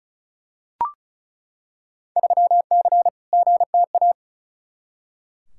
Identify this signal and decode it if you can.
Morse code, heard as a keyed tone, about 36 wpm, 700 Hz. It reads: 3C GTA
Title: Field Day Exchanges Encore - 35wpm